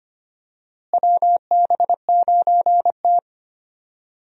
Morse code, heard as a keyed tone, about 25 wpm, 700 Hz. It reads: WB9T
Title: USA Call Signs - 4 Characters - Repeated 2x - 25wpm